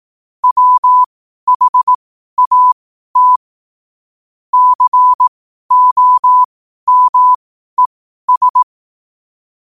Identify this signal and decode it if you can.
Morse code, heard as a keyed tone, about 18 wpm, spaced as for 12 wpm, 1000 Hz. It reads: WHAT COMES
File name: qotdfiles / 12wpm